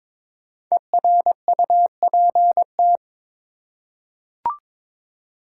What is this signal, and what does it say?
Morse code, heard as a keyed tone, about 21 wpm, 700 Hz. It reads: ERUPT
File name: 5 Letter Words - First 3 Letters Given - 22wpm